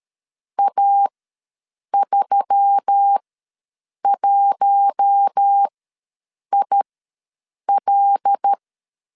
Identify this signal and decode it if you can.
Morse code, heard as a keyed tone, about 13 wpm, spaced as for 7 wpm, 800 Hz. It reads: A31IL
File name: LEZIONI / 5 LBVC / 5 L ws 12wpm 2958